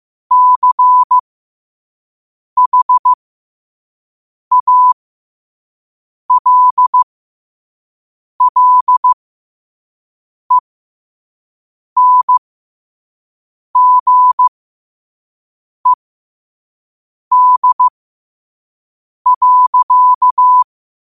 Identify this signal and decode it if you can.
Morse code, heard as a keyed tone, about 15 wpm, spaced as for 5 wpm, 1000 Hz. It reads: CHALLENGED.
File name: qotdfiles / 5wpm